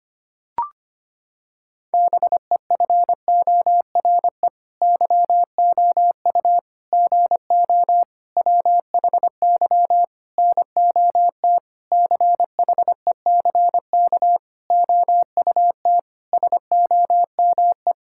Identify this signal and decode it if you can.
Morse code, heard as a keyed tone, about 25 wpm, 700 Hz. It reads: BEFORE YOU GO WHY NOT CHECK OUT SOME
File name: Sets of 9 Words - 25wpm